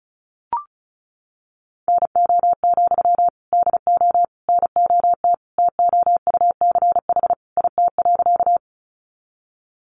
Morse code, heard as a keyed tone, about 34 wpm, 700 Hz. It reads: NO, DO NOT TOUCH IT.
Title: Sentences from Top 1200 Words 35wpm